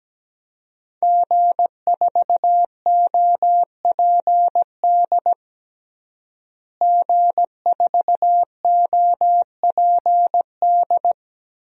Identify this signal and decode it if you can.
Morse code, heard as a keyed tone, about 17 wpm, 700 Hz. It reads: G4OPD G4OPD
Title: DXCC-100 Call Signs - 5 Characters - Repeated 3x - 17wpm